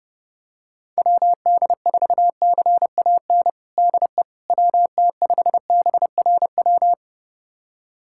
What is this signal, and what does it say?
Morse code, heard as a keyed tone, about 30 wpm, 700 Hz. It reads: WD4CAN DE WT5BRW
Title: USA Callsign DE Callsign 30wpm